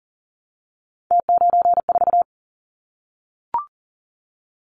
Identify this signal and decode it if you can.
Morse code, heard as a keyed tone, about 42 wpm, 700 Hz. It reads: T94
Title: Callsign Prefixes for Top 100 DXCC - 40wpm